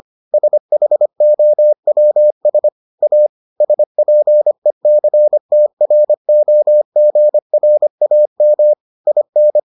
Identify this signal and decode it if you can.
Morse code, heard as a keyed tone, about 25 wpm, 600 Hz. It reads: SHOWS A SPECTROGRAM IN